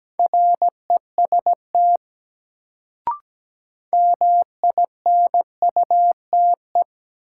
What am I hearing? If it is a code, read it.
Morse code, heard as a keyed tone, about 17 wpm, 700 Hz. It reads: REST MINUTE
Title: Sentences from Top 500 Words - New Word Drill - 17wpm